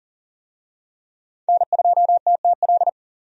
Morse code, heard as a keyed tone, about 40 wpm, 700 Hz. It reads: N2TTL
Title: USA Call Signs - 5 Characters - 40wpm